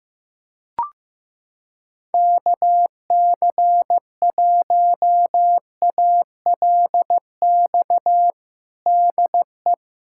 Morse code, heard as a keyed tone, about 16 wpm, 700 Hz. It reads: KC1ALX DE
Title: USA Callsign DE Callsign - 6 Characters - 15wpm